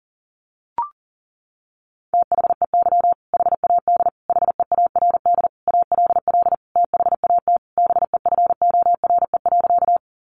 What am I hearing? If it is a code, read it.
Morse code, heard as a keyed tone, about 40 wpm, 700 Hz. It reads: THEY HAD HEARD ALL THAT BEFORE.
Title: Sentences from Top 400 Words 40wpm